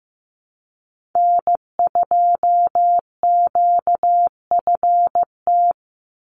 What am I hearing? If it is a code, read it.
Morse code, heard as a keyed tone, about 15 wpm, 700 Hz. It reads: N2QFT
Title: USA Call Signs 15wpm